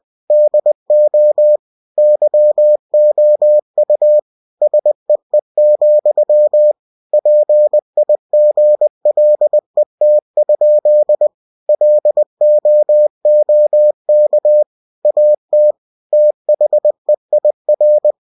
Morse code, heard as a keyed tone, about 20 wpm, 600 Hz. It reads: DO YOU SEE, PIGLET? LOOK AT THEIR
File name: cw-books / winnie-the-pooh / winnie-20wpm-page-0023